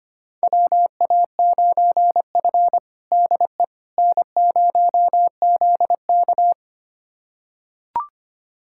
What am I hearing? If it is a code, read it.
Morse code, heard as a keyed tone, about 25 wpm, 700 Hz. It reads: WA9F DE N0ZK
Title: USA Callsign DE Callsign - 4 Characters - 25wpm